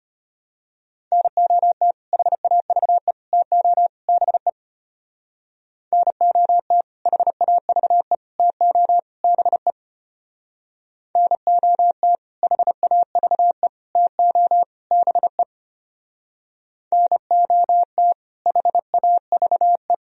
Morse code, heard as a keyed tone, about 31 wpm, 690 Hz. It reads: NOT HAVE TO BE NOT HAVE TO BE NOT HAVE TO BE NOT HAVE
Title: Speed-Racing - Sets of 4 Words 25wpm